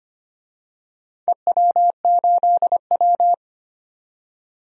Morse code, heard as a keyed tone, about 25 wpm, 700 Hz. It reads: EW8W
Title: DXCC-100 Call Signs - 4 Characters - 25wpm